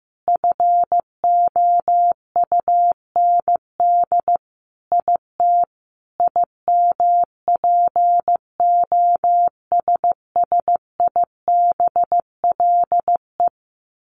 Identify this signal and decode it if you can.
Morse code, heard as a keyed tone, about 15 wpm, 700 Hz. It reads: FOUND IT IMPOSSIBLE
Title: Sentences from Top 1100 Words 15wpm